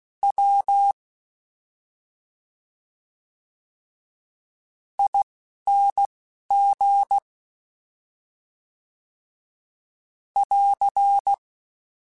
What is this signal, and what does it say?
Morse code, heard as a keyed tone, about 16 wpm, spaced as for 5 wpm, 800 Hz. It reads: WING <AR>
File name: Special INGs-mp3s / Special INGs-mp3s / Slow ing @10